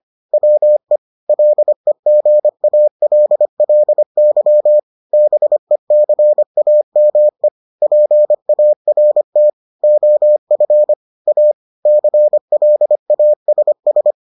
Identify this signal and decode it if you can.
Morse code, heard as a keyed tone, about 25 wpm, 600 Hz. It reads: WE LEGALLY BECAME PART OF A CLASS